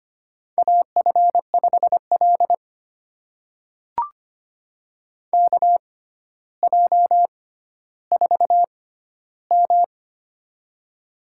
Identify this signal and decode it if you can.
Morse code, heard as a keyed tone, about 25 wpm, spaced as for 13 wpm, 700 Hz. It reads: AF5L KJ4M
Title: USA Call Signs - 4 Characters - 6x Character Spacing - 25wpm